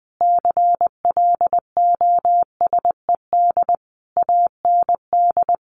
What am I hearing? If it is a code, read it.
Morse code, heard as a keyed tone, about 20 wpm, 700 Hz. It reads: CLOSED AND
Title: Sentences from Top 1400 Words 20wpm